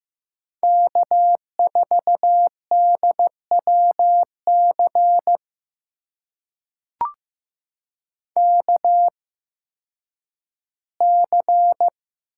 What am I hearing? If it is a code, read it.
Morse code, heard as a keyed tone, about 15 wpm, spaced as for 10 wpm, 700 Hz. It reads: K4DWC KC
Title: USA Call Signs - 8x Character Spacing - 15wpm